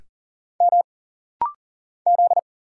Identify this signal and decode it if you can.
Morse code, heard as a keyed tone, about 40 wpm, 700 Hz. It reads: M Z
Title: Single Letter-Number - Rapid-Fire 40wpm